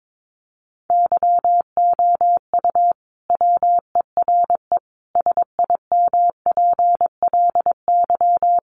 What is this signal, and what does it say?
Morse code, heard as a keyed tone, about 22 wpm, 700 Hz. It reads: YOU WERE SIMPLY
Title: Sentences from Top 1100 Words 22wpm